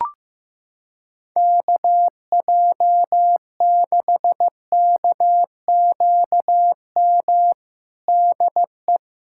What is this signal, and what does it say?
Morse code, heard as a keyed tone, about 15 wpm, 700 Hz. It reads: KJ6KQM DE
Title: Practice QSO - Exchange 4 - Ending - 15wpm